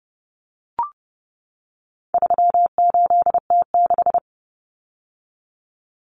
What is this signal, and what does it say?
Morse code, heard as a keyed tone, about 30 wpm, 700 Hz. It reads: 38T6
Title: Common Cut Numbers - 4-Digit 30wpm